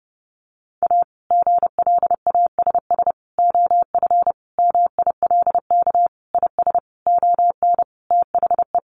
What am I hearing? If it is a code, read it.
Morse code, heard as a keyed tone, about 30 wpm, 700 Hz. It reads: A GLASS OF MILK IS ON THE
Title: Additional Easy Phrases 30wpm